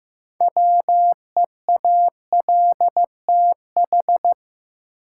Morse code, heard as a keyed tone, about 15 wpm, 700 Hz. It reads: WEALTH
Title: Sentences from Top 1200 Words - New Word Drill Alt - 15wpm